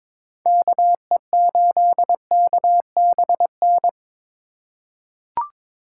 Morse code, heard as a keyed tone, about 21 wpm, 700 Hz. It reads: KE8KBN
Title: USA Call Signs - 2x Character Spacing - 22wpm